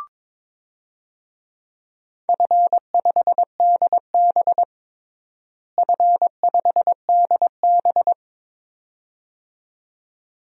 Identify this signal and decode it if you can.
Morse code, heard as a keyed tone, about 22 wpm, 700 Hz. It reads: F5DB F5DB
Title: DXCC-100 Call Signs - 4 Characters - Repeated 2x - 22wpm